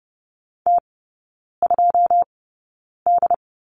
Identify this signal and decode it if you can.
Morse code, heard as a keyed tone, about 30 wpm, spaced as for 9 wpm, 690 Hz. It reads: T2D
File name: Lesson 16e - Letter c - Three Characters 30wpm